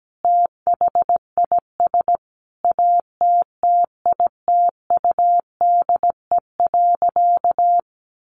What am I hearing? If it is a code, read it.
Morse code, heard as a keyed tone, about 17 wpm, 700 Hz. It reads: THIS ATTITUDE.
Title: Sentences from Top 1300 Words 17wpm